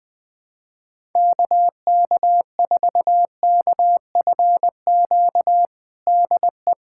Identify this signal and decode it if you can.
Morse code, heard as a keyed tone, about 20 wpm, 700 Hz. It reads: KK4KFQ DE
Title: USA Callsign DE Callsign - 6 Characters - 20wpm